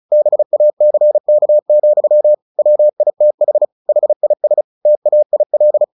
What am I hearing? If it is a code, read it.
Morse code, heard as a keyed tone, about 34 wpm, 600 Hz. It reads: BACK, WITH HIS TAIL
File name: the old man and the sea 2 35wpm